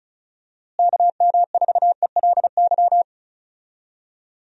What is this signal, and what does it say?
Morse code, heard as a keyed tone, about 34 wpm, 700 Hz. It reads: KM4ELY
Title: USA Call Signs - 6 Characters - Repeated 2x - 35wpm